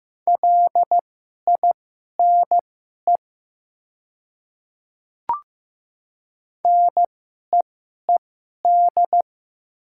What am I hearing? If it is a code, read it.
Morse code, heard as a keyed tone, about 15 wpm, spaced as for 11 wpm, 700 Hz. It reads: LINE NEED
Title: 2x Character Spacing - Top 200 Words - 15wpm